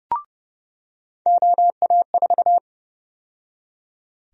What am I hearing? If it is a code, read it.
Morse code, heard as a keyed tone, about 30 wpm, 700 Hz. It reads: OA4
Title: USA Call Sign Prefixes - 30wpm